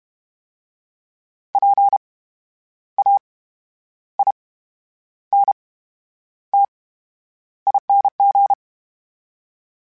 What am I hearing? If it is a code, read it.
Morse code, heard as a keyed tone, about 32 wpm, 800 Hz. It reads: P A I N T ING
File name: Special INGs / Special INGs-mp3s / VerySlowToFaster Ing @30